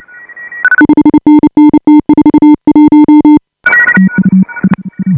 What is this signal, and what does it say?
Morse code, heard as a keyed tone, about 29 wpm, 310 Hz. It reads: 5NN T41
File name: AUDIO2018 / HA5BA CW